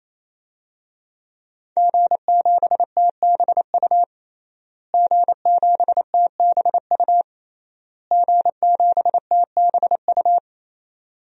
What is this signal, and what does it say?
Morse code, heard as a keyed tone, about 28 wpm, 690 Hz. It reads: G7TBU G7TBU G7TBU
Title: DXCC-100 Call Signs - 5 Characters - Repeated 3x - 28wpm